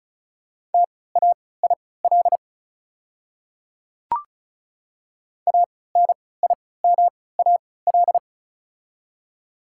Morse code, heard as a keyed tone, about 36 wpm, spaced as for 20 wpm, 690 Hz. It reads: TAIL ANIMAL
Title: 3x Character Spacing - Top 500 Words - 35wpm